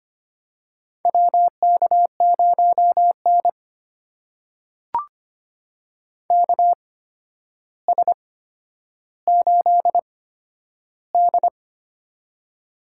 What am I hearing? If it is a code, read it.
Morse code, heard as a keyed tone, about 25 wpm, spaced as for 11 wpm, 700 Hz. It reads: WK0N KS8D
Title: USA Call Signs - 4 Characters - 8x Character Spacing - 25wpm